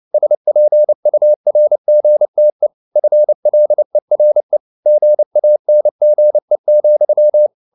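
Morse code, heard as a keyed tone, about 29 wpm, 600 Hz. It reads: SPURGTE FLERE GANGE,
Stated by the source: Fast (QRQ) Morse at 29 wpm, from CW med Gnister.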